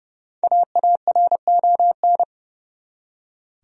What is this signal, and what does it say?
Morse code, heard as a keyed tone, about 30 wpm, 700 Hz. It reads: AARON
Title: USA Common Names - Max 5 Letters - 30wpm